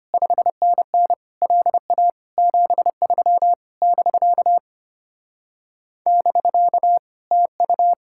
Morse code, heard as a keyed tone, about 33 wpm, 700 Hz. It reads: 5NN LA 73 <BK> <BK> TU